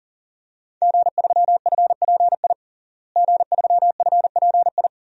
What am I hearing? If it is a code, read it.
Morse code, heard as a keyed tone, about 40 wpm, 700 Hz. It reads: G3FPI G3FPI
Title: DXCC-100 Call Signs - 5 Characters - Repeated 2x - 40wpm